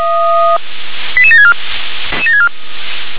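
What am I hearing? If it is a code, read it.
Five-tone selective call (ZVEI), heard as quick runs of five tones, tones of about 70 ms. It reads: 88764
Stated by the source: ZVEI five-tone selective call 88962, received on SDR.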